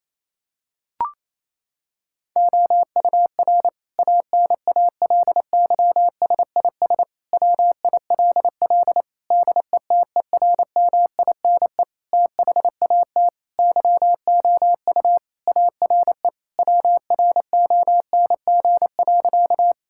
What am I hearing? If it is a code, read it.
Morse code, heard as a keyed tone, about 28 wpm, 700 Hz. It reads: OUR ANALYSIS WILL DETERMINE THAT YOU ARE WRONG.
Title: Sentences from Top 1000 Words - Review Part 5 - 28wpm